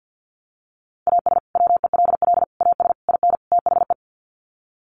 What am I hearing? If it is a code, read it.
Morse code, heard as a keyed tone, about 50 wpm, 700 Hz. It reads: AS WELL AS IN THE
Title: Sets of 5 Words Encore 50wpm